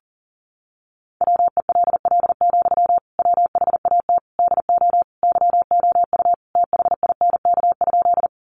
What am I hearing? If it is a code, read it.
Morse code, heard as a keyed tone, about 40 wpm, 700 Hz. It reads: WELL, WHAT DO YOU THINK?
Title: Sentences from Top 200 Words 40wpm